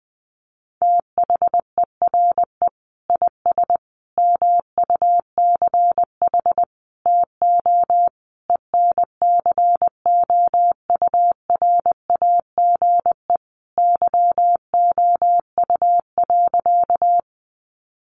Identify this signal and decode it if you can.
Morse code, heard as a keyed tone, about 20 wpm, 700 Hz. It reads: THERE IS MUCH TO ENCOURAGE YOU.
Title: Sentences from Top 1400 Words 20wpm